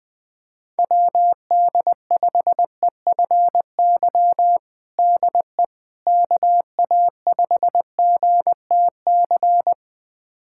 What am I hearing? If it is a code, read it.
Morse code, heard as a keyed tone, about 20 wpm, 700 Hz. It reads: WD5EFY DE KA5GTC
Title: USA Callsign DE Callsign - 6 Characters - 20wpm